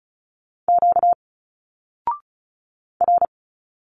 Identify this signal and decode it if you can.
Morse code, heard as a keyed tone, about 35 wpm, 700 Hz. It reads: Q R
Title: Single Letters 35wpm